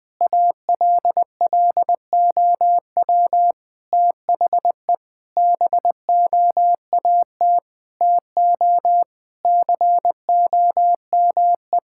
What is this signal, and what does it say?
Morse code, heard as a keyed tone, about 20 wpm, 700 Hz. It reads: ALLOW THE BOAT TO COME
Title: Sentences from Top 1000 Words - Review Part 5 - 20wpm